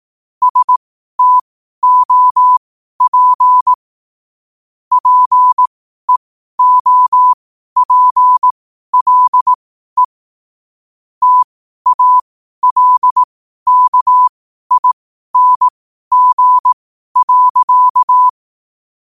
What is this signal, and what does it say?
Morse code, heard as a keyed tone, about 18 wpm, spaced as for 12 wpm, 1000 Hz. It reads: STOP PEOPLE TALKING.
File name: qotdfiles / 12wpm